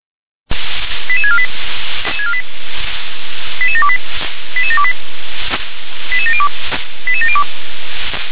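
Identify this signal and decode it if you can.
Five-tone selective call (ZVEI), heard as quick runs of five tones, tones of about 70 ms. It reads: 88648 88728 88728 88702 88702
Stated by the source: ZVEI five-tone selective call 88731, received by SDR.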